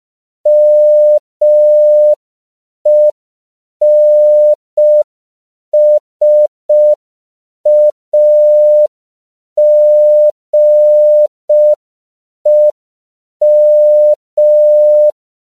Morse code, heard as a keyed tone, about 5 wpm, 600 Hz. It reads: MENSAGEM